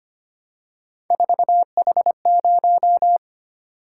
Morse code, heard as a keyed tone, about 29 wpm, 700 Hz. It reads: 4H0
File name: Speed-Racing - Three Letter-Number Combinations - 25wpm